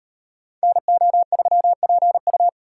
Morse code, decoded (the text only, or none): NO3PU